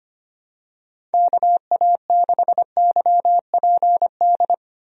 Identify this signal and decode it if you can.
Morse code, heard as a keyed tone, about 25 wpm, 700 Hz. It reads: KA6YPD
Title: USA Call Signs - 6 Characters - 25wpm